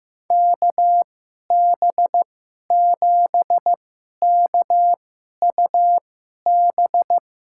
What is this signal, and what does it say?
Morse code, heard as a keyed tone, about 15 wpm, spaced as for 11 wpm, 700 Hz. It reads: KB7KUB
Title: USA Call Signs - 6 Characters - 2x Character Spacing - 15wpm